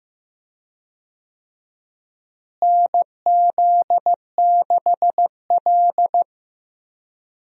Morse code, heard as a keyed tone, about 15 wpm, 700 Hz. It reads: NZ6L
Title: USA Call Signs - 4 Characters - Repeated 2x - 15wpm